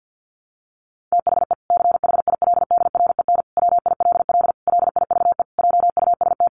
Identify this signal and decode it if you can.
Morse code, heard as a keyed tone, about 50 wpm, 700 Hz. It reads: THE CHILDREN WILL LIVE JUST
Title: Sentences from Top 300 Words 50wpm